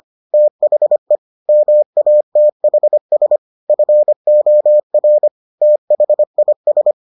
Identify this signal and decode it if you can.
Morse code, heard as a keyed tone, about 25 wpm, 600 Hz. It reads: THE MATHS FOR THIS